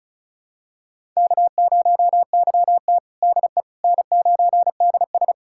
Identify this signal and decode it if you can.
Morse code, heard as a keyed tone, about 34 wpm, 700 Hz. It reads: K0YT DE N9DS
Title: USA Callsign DE Callsign - 4 Characters - 35wpm